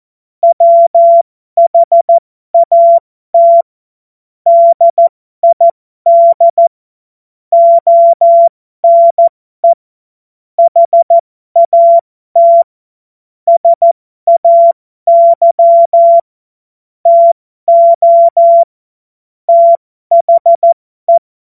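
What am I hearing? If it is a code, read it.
Morse code, heard as a keyed tone, about 14 wpm, spaced as for 12 wpm, 680 Hz. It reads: WHAT DID ONE HAT SAY TO THE
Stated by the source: Morse code joke.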